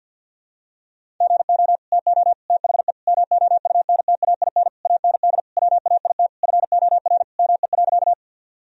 Morse code, heard as a keyed tone, about 51 wpm, 700 Hz. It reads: GO TO THE MOUNTAIN AND WAIT FOR ME.
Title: Sentences from Top 1000 Words - Review Part 1 - 50wpm